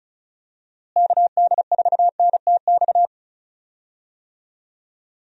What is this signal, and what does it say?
Morse code, heard as a keyed tone, about 36 wpm, 700 Hz. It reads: KD4NTX
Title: USA Call Signs - 6 Characters - Repeated 2x - 35wpm